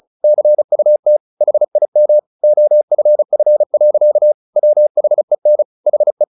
Morse code, heard as a keyed tone, about 34 wpm, 600 Hz. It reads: CUT HIM OFF. WHEN HE
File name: the old man and the sea 2 35wpm